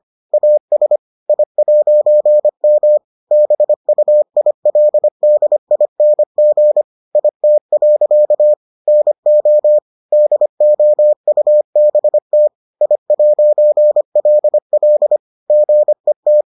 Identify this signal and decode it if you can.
Morse code, heard as a keyed tone, about 25 wpm, 600 Hz. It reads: AS I'M BUILDING IT. NO DOUBT I'LL GET